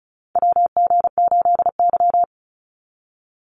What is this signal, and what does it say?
Morse code, heard as a keyed tone, about 36 wpm, 700 Hz. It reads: WG8Y
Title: NA SOTA Chaser Callsigns 35wpm